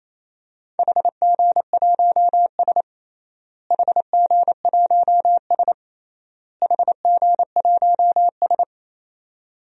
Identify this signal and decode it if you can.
Morse code, heard as a keyed tone, about 28 wpm, 700 Hz. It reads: HG1S HG1S HG1S
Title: DXCC-100 Call Signs - 4 Characters - Repeated 3x - 28wpm